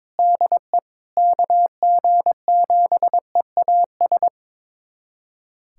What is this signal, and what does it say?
Morse code, heard as a keyed tone, about 22 wpm, 700 Hz. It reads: DE KG7EAS
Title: USA Callsign DE Callsign - 6 Characters - 22wpm